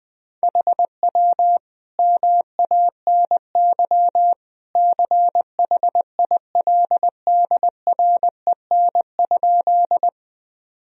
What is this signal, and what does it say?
Morse code, heard as a keyed tone, about 20 wpm, 700 Hz. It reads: HW MANY CHILDREN?
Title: Sentences from Top 300 Words and Abbreviations 20wpm